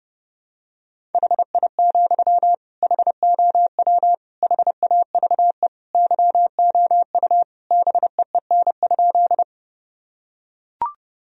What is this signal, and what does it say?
Morse code, heard as a keyed tone, about 30 wpm, 700 Hz. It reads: HI, HOW HAVE YOU BEEN?
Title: Sentences from Top 1000 Words - Review Part 3 - 30wpm